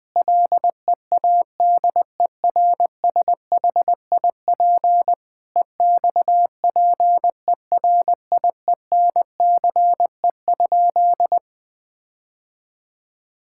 Morse code, heard as a keyed tone, about 21 wpm, 700 Hz. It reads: LEADERSHIP EXPERIENCE?
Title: Sentences from Top 1600 Words 20wpm